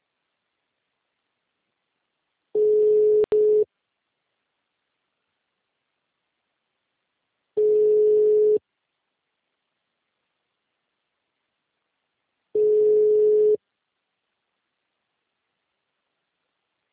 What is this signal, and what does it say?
Ringback tone (Europe, 425 Hz).
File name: out- -2069- - - .20029